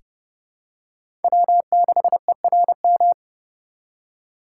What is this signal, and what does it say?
Morse code, heard as a keyed tone, about 30 wpm, 700 Hz. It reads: W6ERM
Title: USA Call Signs - 5 Characters - 3x Character Spacing - 30wpm